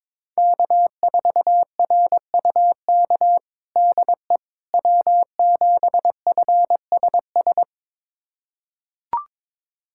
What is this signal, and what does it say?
Morse code, heard as a keyed tone, about 22 wpm, 700 Hz. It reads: K4RUK DE W7FSS
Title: USA Callsign DE Callsign - 5 Characters - 22wpm